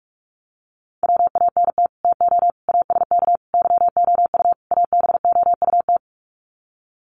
Morse code, heard as a keyed tone, about 44 wpm, 700 Hz. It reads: WANT TO ASK YOU ABOUT
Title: Sets of 5 Words Encore2 45wpm